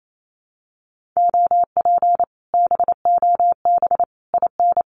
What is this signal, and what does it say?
Morse code, heard as a keyed tone, about 28 wpm, 700 Hz. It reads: OP BOB IN
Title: Common QSO Phrases Encore 28wpm